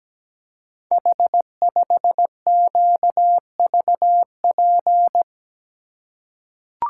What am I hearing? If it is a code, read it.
Morse code, heard as a keyed tone, about 17 wpm, 700 Hz. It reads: H5QVP
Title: Code Groups - 5 Letter:Numbers - 17wpm